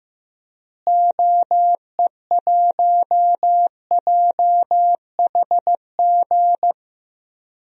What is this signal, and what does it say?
Morse code, heard as a keyed tone, about 16 wpm, 700 Hz. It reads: OE1JHG
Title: DXCC-100 Call Signs - 6 Characters - Repeated 3x - 15wpm